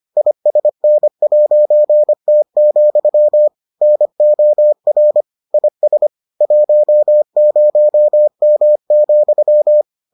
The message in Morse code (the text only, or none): ISN'T, NOR IS 10M,